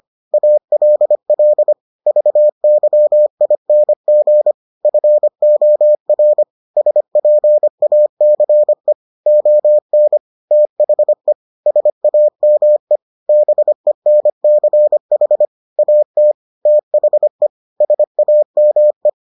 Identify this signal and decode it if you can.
Morse code, heard as a keyed tone, about 25 wpm, 600 Hz. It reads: ALL VYING FOR SPACE ON THE SAME BENCH AT THE SAME